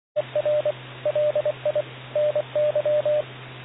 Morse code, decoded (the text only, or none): ERLI NY